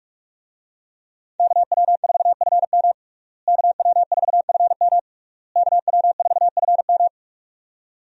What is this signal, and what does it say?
Morse code, heard as a keyed tone, about 41 wpm, 700 Hz. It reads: KW4FM KW4FM KW4FM
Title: USA Call Signs - 5 Characters - Repeated 3x - 45wpm